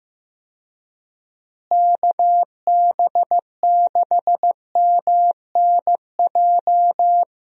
Morse code, heard as a keyed tone, about 15 wpm, 700 Hz. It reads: KB6MNJ
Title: USA Call Signs - 6 Characters - Repeated 2x - 15wpm